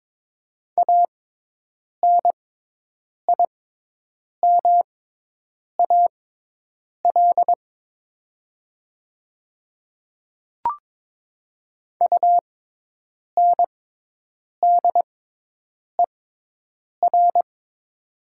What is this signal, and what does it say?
Morse code, heard as a keyed tone, about 22 wpm, spaced as for 8 wpm, 700 Hz. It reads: ANIMAL UNDER
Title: 6x Character Spacing - Top 200 Words - 22wpm